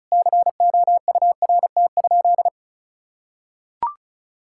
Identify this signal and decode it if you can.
Morse code, heard as a keyed tone, about 36 wpm, 700 Hz. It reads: COURT?
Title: Sentences from Top 1600 Words 35wpm